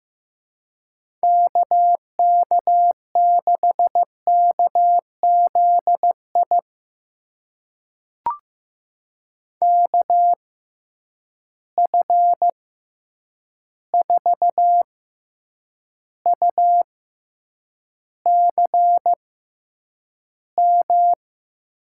Morse code, heard as a keyed tone, about 15 wpm, spaced as for 8 wpm, 700 Hz. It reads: KK6KZI KF4UCM
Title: USA Call Signs - 6x Character Spacing - 15wpm